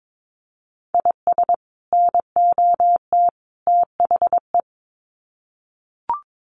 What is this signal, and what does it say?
Morse code, heard as a keyed tone, about 22 wpm, 700 Hz. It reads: IS NOT THE
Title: Sets of 3 Words 22wpm